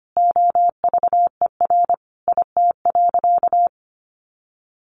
Morse code, heard as a keyed tone, about 25 wpm, 700 Hz. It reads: OVER IT.
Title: Sentences from Top 1500 Words 25wpm